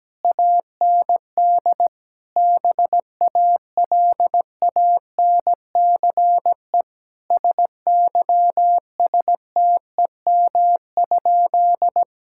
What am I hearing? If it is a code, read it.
Morse code, heard as a keyed tone, about 17 wpm, 700 Hz. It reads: AND BALANCE SYSTEM?
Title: Sentences from Top 1000 Words - Review Part 6 - 17wpm